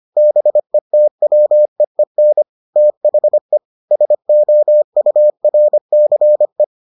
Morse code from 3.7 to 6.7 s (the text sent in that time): SOURCE